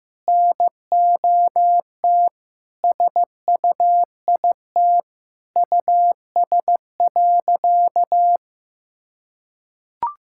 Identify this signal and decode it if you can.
Morse code, heard as a keyed tone, about 15 wpm, 700 Hz. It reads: NOT SUIT US.
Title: Sentences from Top 1600 Words 15wpm